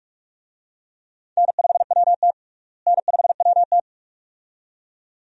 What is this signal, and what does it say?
Morse code, heard as a keyed tone, about 46 wpm, 700 Hz. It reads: N5WT N5WT
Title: USA Call Signs - 4 Characters - Repeated 2x - 45wpm